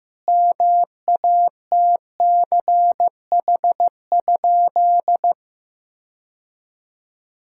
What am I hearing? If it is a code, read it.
Morse code, heard as a keyed tone, about 15 wpm, 700 Hz. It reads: MATCH?
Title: Sentences from Top 1600 Words 15wpm